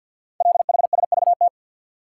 Morse code, heard as a keyed tone, about 48 wpm, 700 Hz. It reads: LHSVT
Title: Code Groups - 5 Letters - 50wpm